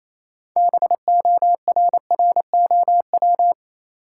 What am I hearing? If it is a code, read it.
Morse code, heard as a keyed tone, about 28 wpm, 700 Hz. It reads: BORROW
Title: Sentences from Top 1500 Words - New Word Drill - 28wpm